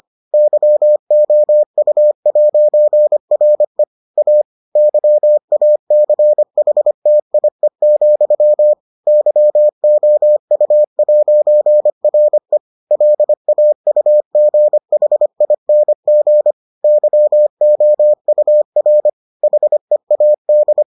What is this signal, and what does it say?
Morse code, heard as a keyed tone, about 25 wpm, 600 Hz. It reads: YOU'RE A YACHTIE, YOU'RE LAUGHING YOUR HEAD